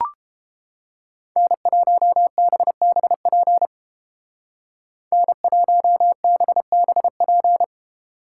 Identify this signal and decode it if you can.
Morse code, heard as a keyed tone, about 31 wpm, 700 Hz. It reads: N1BBP N1BBP